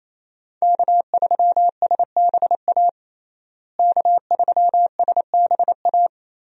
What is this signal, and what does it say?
Morse code, heard as a keyed tone, about 28 wpm, 700 Hz. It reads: K3SBA K3SBA
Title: USA Call Signs - 5 Characters - Repeated 2x - 28wpm